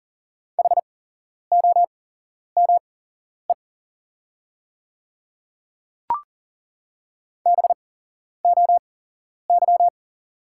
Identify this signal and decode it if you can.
Morse code, heard as a keyed tone, about 40 wpm, spaced as for 11 wpm, 700 Hz. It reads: HOME BOY
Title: 8x Character Spacing - Top 200 Words - 40wpm